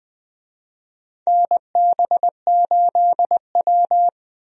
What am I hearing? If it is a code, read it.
Morse code, heard as a keyed tone, about 20 wpm, 700 Hz. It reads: NB8W
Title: USA Call Signs - 4 Characters - Repeated 2x - 20wpm